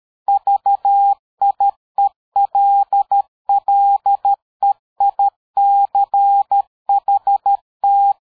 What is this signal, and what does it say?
Morse code, heard as a keyed tone, about 13 wpm, 800 Hz. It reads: VIELLEICHT
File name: dk5ke / PersWorte 60